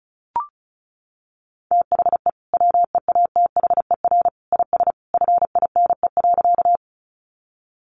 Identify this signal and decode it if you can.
Morse code, heard as a keyed tone, about 34 wpm, 700 Hz. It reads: THE WEATHER IS FINE.
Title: Sentences from Top 800 Words 35wpm